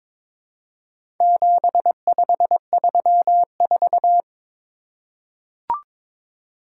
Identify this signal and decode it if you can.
Morse code, heard as a keyed tone, about 22 wpm, 700 Hz. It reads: 7534
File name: practice / 22wpm / Four Letter-Number Combinations - 22wpm